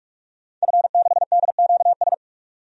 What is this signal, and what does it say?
Morse code, heard as a keyed tone, about 46 wpm, 700 Hz. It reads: F6DQS
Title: DXCC-100 Call Signs - 5 Characters - 45wpm